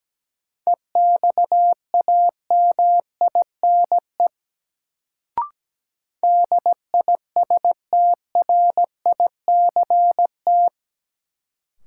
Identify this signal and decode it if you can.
Morse code, heard as a keyed tone, about 17 wpm, 700 Hz. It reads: EXAMINE DISTRICT
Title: Sentences from Top 1600 Words - New Word Drill Alt - 17wpm